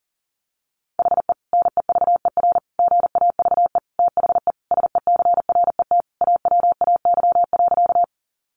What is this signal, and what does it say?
Morse code, heard as a keyed tone, about 40 wpm, 700 Hz. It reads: HE NEVER GAVE THE SECRET AWAY.
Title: Sentences from Top 1200 Words 40wpm